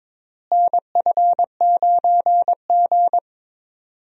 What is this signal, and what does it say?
Morse code, heard as a keyed tone, about 22 wpm, 700 Hz. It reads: NF9G
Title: USA Call Signs - 4 Characters - Repeated 3x - 22wpm